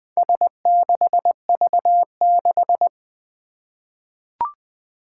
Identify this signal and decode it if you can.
Morse code, heard as a keyed tone, about 21 wpm, 700 Hz. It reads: S6V6
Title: Four Challenging Letter-Number Combinations 20wpm